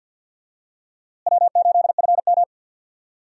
Morse code, heard as a keyed tone, about 52 wpm, 700 Hz. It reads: W8FM
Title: USA Call Signs - 4 Characters - Repeated 2x - 50wpm